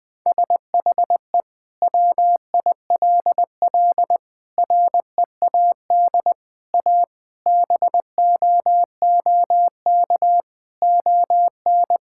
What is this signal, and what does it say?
Morse code, heard as a keyed tone, about 20 wpm, 700 Hz. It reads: SHE WILL READ A BOOK ON
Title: Sentences from Top 1000 Words - Review Part 6 - 20wpm